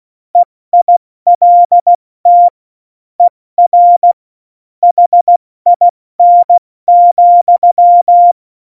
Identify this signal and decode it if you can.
Morse code, heard as a keyed tone, about 16 wpm, 700 Hz. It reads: EILT ER HIN,
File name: Morsekurs / Witze / Witz 13. Oktober 2020, 70 F80